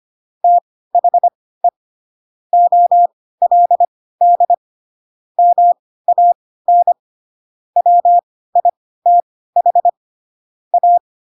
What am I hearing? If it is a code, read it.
Morse code, heard as a keyed tone, about 25 wpm, spaced as for 16 wpm, 700 Hz. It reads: THE OLD MAN WITH A
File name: Pinocchio / Pinocchio ch7 10 25 ik1hge